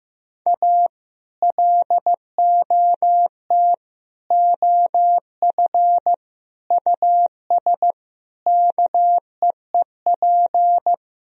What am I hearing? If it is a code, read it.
Morse code, heard as a keyed tone, about 15 wpm, 700 Hz. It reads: A LOT OF US KEEP
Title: Sentences from Top 1000 Words - Review Part 3 - 15wpm